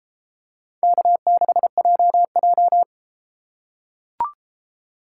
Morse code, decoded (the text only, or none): K6JJ